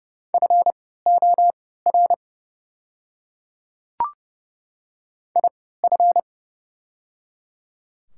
Morse code, heard as a keyed tone, about 30 wpm, spaced as for 17 wpm, 690 Hz. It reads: FOR IF
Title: 3x Character Spacing - Top 100 Words - 30wpm